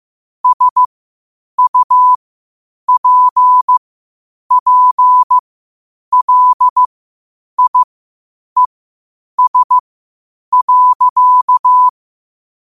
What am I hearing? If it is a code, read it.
Morse code, heard as a keyed tone, about 15 wpm, spaced as for 8 wpm, 1000 Hz. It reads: SUPPLIES.
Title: 8 WPM morse code quotes for Fri, 01 Mar 2024